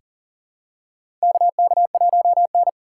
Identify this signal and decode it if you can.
Morse code, heard as a keyed tone, about 40 wpm, 700 Hz. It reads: KK1N
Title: USA Call Signs - 3x Character Spacing - 40wpm